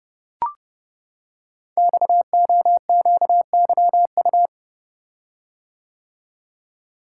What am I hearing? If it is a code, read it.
Morse code, heard as a keyed tone, about 30 wpm, 700 Hz. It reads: XOQYU
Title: Code Groups - 5 Letters - 30wpm